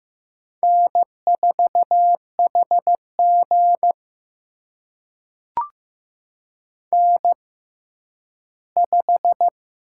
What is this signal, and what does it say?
Morse code, heard as a keyed tone, about 15 wpm, spaced as for 10 wpm, 700 Hz. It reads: N4HG N5
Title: USA Call Signs - 6x Character Spacing - 15wpm